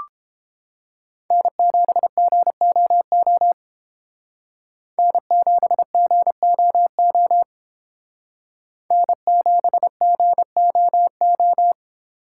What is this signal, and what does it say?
Morse code, heard as a keyed tone, about 29 wpm, 700 Hz. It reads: N7GOO N7GOO N7GOO